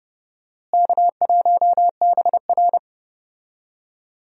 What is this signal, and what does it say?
Morse code, heard as a keyed tone, about 31 wpm, 700 Hz. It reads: K1BR
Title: USA Call Signs - 4 Characters - Repeated 3x - 30wpm